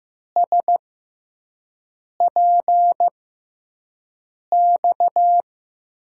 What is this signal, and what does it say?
Morse code, heard as a keyed tone, about 15 wpm, spaced as for 5 wpm, 700 Hz. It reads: SPX
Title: USA Call Signs - 6x Character Spacing - 15wpm